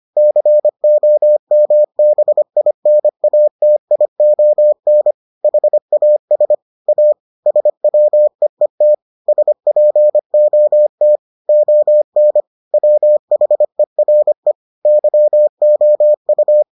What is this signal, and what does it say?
Morse code, heard as a keyed tone, about 25 wpm, 600 Hz. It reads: COMBINATION HAS A SWEET SPOT ON WHERE YOU